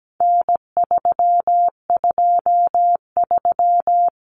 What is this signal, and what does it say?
Morse code, heard as a keyed tone, about 17 wpm, 700 Hz. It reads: N323
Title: Common Cut Numbers - 4-Digit 17wpm